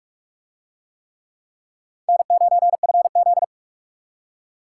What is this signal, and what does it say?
Morse code, heard as a keyed tone, about 41 wpm, 700 Hz. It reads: N9FZ